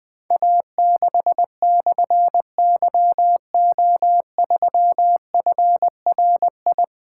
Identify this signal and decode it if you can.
Morse code, heard as a keyed tone, about 20 wpm, 700 Hz. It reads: A6/YO3FRI
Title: Difficult Call Signs 20wpm